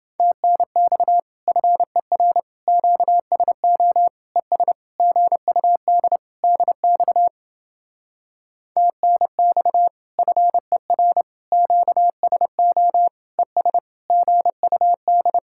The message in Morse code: TNX FER QSO ES GUD DX TNX FER QSO ES GUD

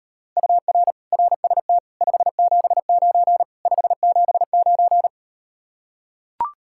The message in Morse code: UR RST 579 579

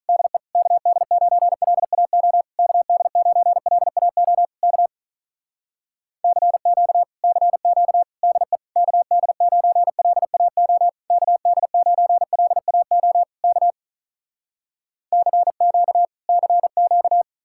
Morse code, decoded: DE KD9LAO KD9LAO K CQ CQ DE KD9LAO KD9LAO K CQ CQ

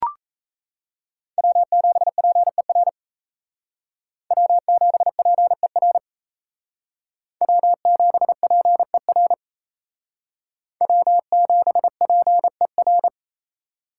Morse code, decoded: W7PER W7PER W7PER W7PER

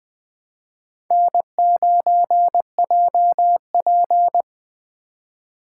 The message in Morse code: N9JP

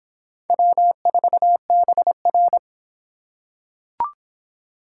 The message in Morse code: W4BR

